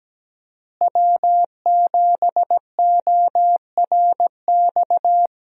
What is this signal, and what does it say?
Morse code, heard as a keyed tone, about 17 wpm, 700 Hz. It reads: W7ORX
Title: USA Call Signs - 5 Characters - Repeated 2x - 17wpm